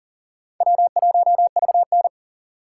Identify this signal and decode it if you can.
Morse code, heard as a keyed tone, about 39 wpm, 700 Hz. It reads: W1VN
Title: USA Call Signs - 4 Characters - 6x Character Spacing - 40wpm